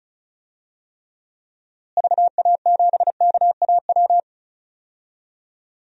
Morse code, heard as a keyed tone, about 36 wpm, 700 Hz. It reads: VA7KAW
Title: DXCC-100 Call Signs - 6 Characters - Repeated 2x - 35wpm